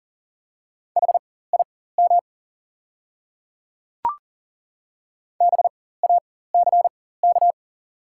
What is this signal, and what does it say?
Morse code, heard as a keyed tone, about 40 wpm, spaced as for 19 wpm, 690 Hz. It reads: HIM BACK
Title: 4x Character Spacing - Top 300 Words - 40wpm